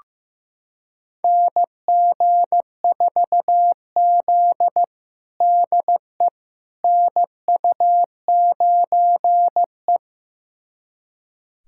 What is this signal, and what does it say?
Morse code, heard as a keyed tone, about 15 wpm, 700 Hz. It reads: NG4Z DE NU9E
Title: USA Callsign DE Callsign - 4 Characters - 15wpm